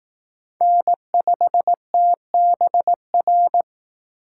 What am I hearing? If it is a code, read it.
Morse code, heard as a keyed tone, about 18 wpm, 700 Hz. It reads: N5TBR
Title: Speed Racing - USA 1x3 Call Signs 15wpm